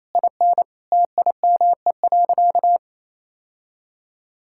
Morse code, heard as a keyed tone, about 28 wpm, 700 Hz. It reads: IN TIME.
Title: Sentences from Top 1000 Words - Review Part 6 - 28wpm